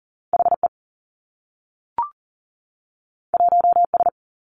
Morse code, heard as a keyed tone, about 38 wpm, 700 Hz. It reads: H E 1 S
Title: Two Challenging Letter-Number Combinations 40wpm